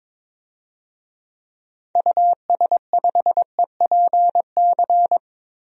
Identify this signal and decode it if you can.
Morse code, heard as a keyed tone, about 22 wpm, 700 Hz. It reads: US5EPC
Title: DXCC-100 Call Signs - 6 Characters - Repeated 3x - 22wpm